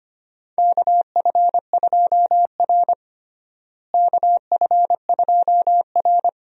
Morse code, heard as a keyed tone, about 25 wpm, 700 Hz. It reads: KF2R KF2R
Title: USA Call Signs - 4 Characters - Repeated 2x - 25wpm